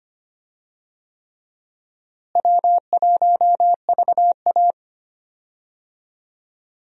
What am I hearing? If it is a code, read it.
Morse code, heard as a keyed tone, about 25 wpm, 700 Hz. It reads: W1VA
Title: USA Call Signs - 4 Characters - Repeated 3x - 25wpm